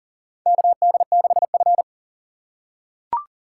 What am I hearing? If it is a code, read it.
Morse code, heard as a keyed tone, about 40 wpm, 700 Hz. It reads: KD6F
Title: USA Call Signs - 4 Characters - 3x Character Spacing - 40wpm